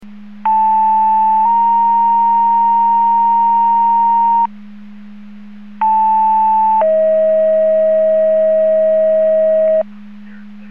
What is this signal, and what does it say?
Two-tone page, heard as a one-second tone, then a three-second tone, about 880 then 930 Hz, about 880 then 640 Hz. It is heard from 0.5 to 9.8 s.
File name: Portage 154.385 Paging